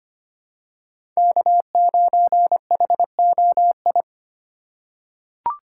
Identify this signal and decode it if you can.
Morse code, heard as a keyed tone, about 25 wpm, 700 Hz. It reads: K9HOI
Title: USA Call Signs - 5 Characters - 3x Character Spacing - 25wpm